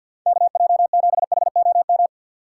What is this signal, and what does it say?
Morse code, heard as a keyed tone, about 53 wpm, 700 Hz. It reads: KJ7HOM
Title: USA Call Signs - 6 Characters - 6x Character Spacing - 50wpm